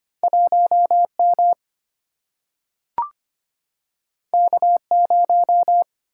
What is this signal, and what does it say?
Morse code, heard as a keyed tone, about 25 wpm, 700 Hz. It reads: 1 M K 0
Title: Single-Letter Single-Number 25wpm